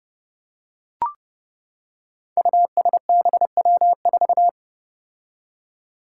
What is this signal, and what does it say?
Morse code, heard as a keyed tone, about 30 wpm, 700 Hz. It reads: USBW4
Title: Five Challenging Letter-Number Combinations 30wpm